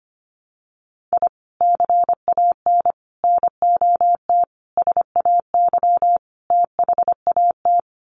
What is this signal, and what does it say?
Morse code, heard as a keyed tone, about 25 wpm, 700 Hz. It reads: I CAN NOT SAY THAT
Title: Sets of 5 Words Encore2 25wpm